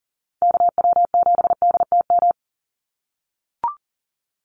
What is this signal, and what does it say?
Morse code, heard as a keyed tone, about 40 wpm, 700 Hz. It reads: KW7DTM
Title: USA Call Signs 40wpm